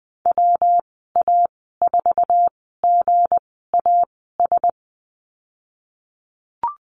Morse code, heard as a keyed tone, about 20 wpm, spaced as for 15 wpm, 700 Hz. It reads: WA4GAS
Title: 193 Lesson 31g - Letter v - Call Signs 20wpm